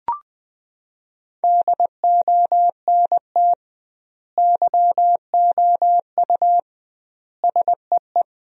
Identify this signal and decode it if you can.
Morse code, heard as a keyed tone, about 20 wpm, 700 Hz. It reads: DONT YOU SEE
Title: 2x Word-Spacing - Sentences from Top 300 Words - 20wpm